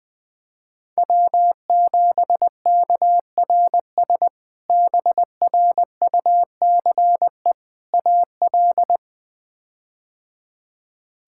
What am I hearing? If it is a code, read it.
Morse code, heard as a keyed tone, about 20 wpm, 700 Hz. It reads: W7KRS BRUCE AL
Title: K1USN SST QSO Practice - 20wpm